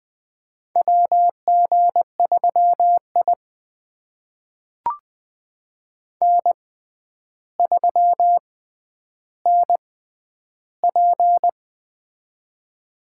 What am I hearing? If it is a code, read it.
Morse code, heard as a keyed tone, about 20 wpm, spaced as for 10 wpm, 700 Hz. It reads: WG3I N3NP
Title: USA Call Signs - 4 Characters - 6x Character Spacing - 20wpm